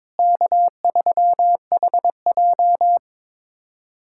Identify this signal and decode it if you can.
Morse code, heard as a keyed tone, about 22 wpm, 700 Hz. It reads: K3HJ
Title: USA Callsign DE Callsign - 4 Characters - 22wpm